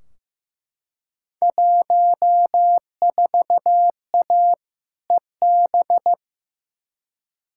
Morse code, heard as a keyed tone, about 15 wpm, 700 Hz. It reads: 14A EB
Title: Field Day Exchanges Encore - 15wpm